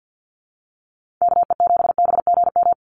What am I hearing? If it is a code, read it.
Morse code, heard as a keyed tone, about 50 wpm, 700 Hz. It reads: KE7BGM
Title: NA SOTA Activator Callsigns 50wpm